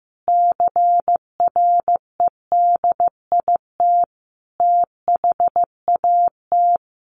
Morse code, heard as a keyed tone, about 15 wpm, 700 Hz. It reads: CREDIT THAT
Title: Sentences from Top 1400 Words 15wpm